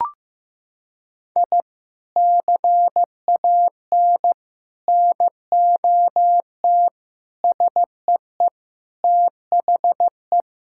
Morse code, heard as a keyed tone, about 15 wpm, 700 Hz. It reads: I CAN NOT SEE THE
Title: Sentences from Top 1000 Words - Review Part 6 - 15wpm